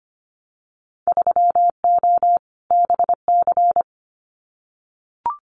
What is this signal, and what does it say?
Morse code, heard as a keyed tone, about 25 wpm, 700 Hz. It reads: 3O BC
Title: Winter Field Day Exchanges Encore 25wpm